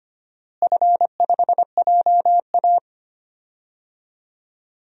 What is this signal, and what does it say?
Morse code, heard as a keyed tone, about 25 wpm, 700 Hz. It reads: F5JA
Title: DXCC-100 Call Signs - 4 Characters - Repeated 3x - 25wpm